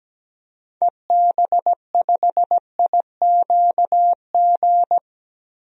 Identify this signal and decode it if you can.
Morse code, heard as a keyed tone, about 17 wpm, 700 Hz. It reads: EB5IQG
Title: DXCC-100 Call Signs - 6 Characters - 17wpm